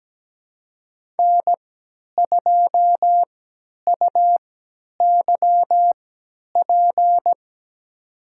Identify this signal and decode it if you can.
Morse code, heard as a keyed tone, about 17 wpm, spaced as for 10 wpm, 700 Hz. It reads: N2UYP
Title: USA Call Signs - 5 Characters - 3x Character Spacing - 17wpm